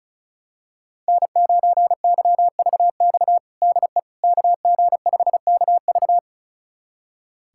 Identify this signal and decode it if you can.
Morse code, heard as a keyed tone, about 36 wpm, 690 Hz. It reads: N9YVX DE KG5KV
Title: USA Callsign DE Callsign - 5 Characters - 35wpm